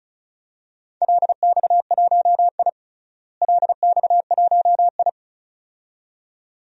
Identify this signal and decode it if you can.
Morse code, heard as a keyed tone, about 36 wpm, 700 Hz. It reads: LX1I LX1I
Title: DXCC-100 Call Signs - 4 Characters - Repeated 2x - 35wpm